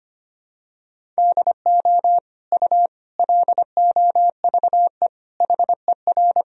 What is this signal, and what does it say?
Morse code, heard as a keyed tone, about 25 wpm, 700 Hz. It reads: DO U LOVE HER
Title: Sentences from Top 400 Words and Abbreviations 25wpm